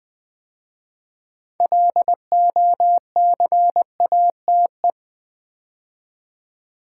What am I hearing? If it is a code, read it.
Morse code, heard as a keyed tone, about 20 wpm, 700 Hz. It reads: LOCATE
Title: 6 Letter Words - First 3 Letters Given - 20wpm